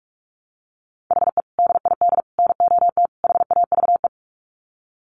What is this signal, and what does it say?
Morse code, heard as a keyed tone, about 44 wpm, 700 Hz. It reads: HE DID NOT HAVE
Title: Sets of 4 Words 45wpm